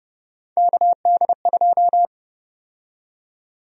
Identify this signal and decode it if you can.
Morse code, heard as a keyed tone, about 30 wpm, 700 Hz. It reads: KD2
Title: USA Call Sign Prefixes - Repeated 2x - 30wpm